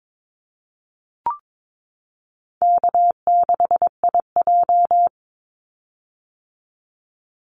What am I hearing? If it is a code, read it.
Morse code, heard as a keyed tone, about 22 wpm, 700 Hz. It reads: K6IJ
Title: Easy Call Signs - 22wpm